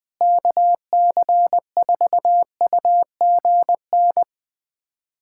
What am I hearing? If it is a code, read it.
Morse code, heard as a keyed tone, about 20 wpm, 700 Hz. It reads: KC4UGN
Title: USA Callsign DE Callsign 20wpm